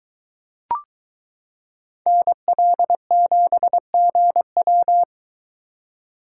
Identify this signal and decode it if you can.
Morse code, heard as a keyed tone, about 22 wpm, 700 Hz. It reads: NL7GW